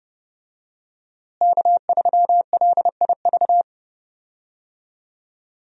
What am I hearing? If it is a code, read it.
Morse code, heard as a keyed tone, about 30 wpm, 700 Hz. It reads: K3LIV
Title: USA Call Signs - 5 Characters - Repeated 3x - 30wpm